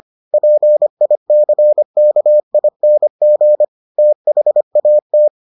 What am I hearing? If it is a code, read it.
Morse code, heard as a keyed tone, about 25 wpm, 600 Hz. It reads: PICKING THAT